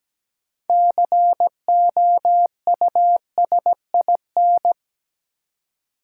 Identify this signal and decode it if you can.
Morse code, heard as a keyed tone, about 17 wpm, 700 Hz. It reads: COUSIN
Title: Six Letter Words - 17wpm